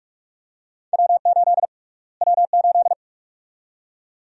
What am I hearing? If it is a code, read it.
Morse code, heard as a keyed tone, about 43 wpm, 700 Hz. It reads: W8 W8
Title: USA Call Sign Prefixes - Repeated 2x - 45wpm